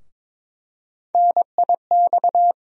Morse code, heard as a keyed tone, about 22 wpm, 700 Hz. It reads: NIX
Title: USA Call Sign Postfixes - 3 Letters 22wpm